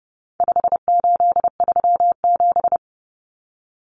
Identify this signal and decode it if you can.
Morse code, heard as a keyed tone, about 30 wpm, 700 Hz. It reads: 5837
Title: Numbers - 4-Digit 30wpm